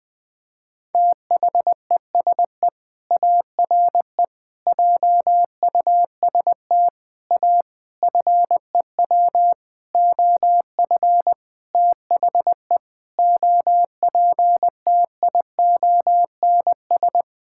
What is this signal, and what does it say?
Morse code, heard as a keyed tone, about 21 wpm, 700 Hz. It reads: THESE ARE JUST A FEW OF THE OPTIONS
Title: Sentences with 7-Word Phrases - Spoken Beginning - 20wpm